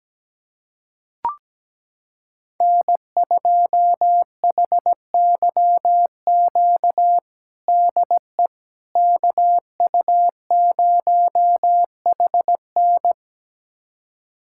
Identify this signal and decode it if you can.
Morse code, heard as a keyed tone, about 17 wpm, 700 Hz. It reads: N2HYQ DE KU0HN
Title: USA Callsign DE Callsign - 5 Characters - 17wpm